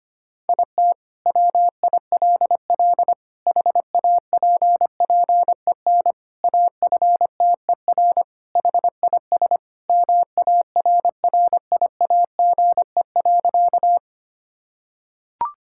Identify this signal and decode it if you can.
Morse code, heard as a keyed tone, about 25 wpm, 700 Hz. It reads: IT WILL HAPPEN AFTER HIS MARRIAGE.
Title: Sentences from Top 1000 Words - Review Part 5 - 25wpm